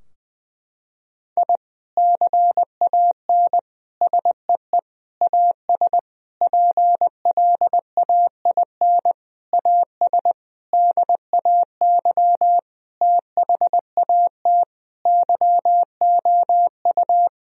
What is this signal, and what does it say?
Morse code, heard as a keyed tone, about 20 wpm, 700 Hz. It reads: I CAN SEE AS PLAIN AS DAY THAT YOU
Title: Sentences from Top 1000 Words - Review Part 2 - 20wpm